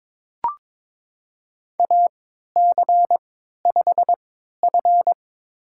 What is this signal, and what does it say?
Morse code, heard as a keyed tone, about 22 wpm, spaced as for 12 wpm, 700 Hz. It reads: AC5F
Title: USA Call Signs - 4 Characters - 3x Character Spacing - 22wpm